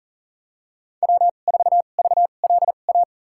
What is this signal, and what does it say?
Morse code, heard as a keyed tone, about 37 wpm, spaced as for 28 wpm, 700 Hz. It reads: W4VLA
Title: USA Call Signs - 5 Characters - 2x Character Spacing - 40wpm